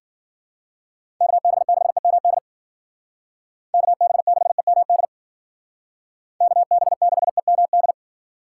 Morse code, heard as a keyed tone, about 49 wpm, 700 Hz. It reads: KB6EMD KB6EMD KB6EMD
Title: Speed Racing - USA 2x3 Call Signs 40wpm